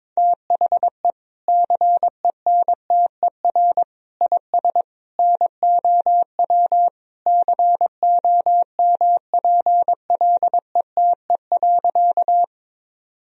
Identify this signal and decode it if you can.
Morse code, heard as a keyed tone, about 22 wpm, 700 Hz. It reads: THE CENTER IS NOW COMPLETE.
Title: Sentences from Top 1000 Words - Review Part 2 - 22wpm